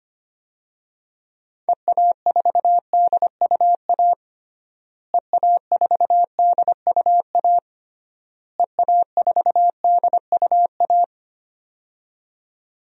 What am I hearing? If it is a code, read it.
Morse code, heard as a keyed tone, about 25 wpm, 700 Hz. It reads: EA4DUA EA4DUA EA4DUA
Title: DXCC-100 Call Signs - 6 Characters - Repeated 3x - 25wpm